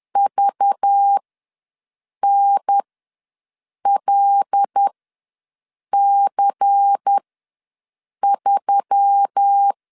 Morse code, heard as a keyed tone, about 10 wpm, spaced as for 6 wpm, 800 Hz. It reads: VNLC3
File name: LEZIONI / 5 LBVC / 5 LBVC ws 10wpm 3733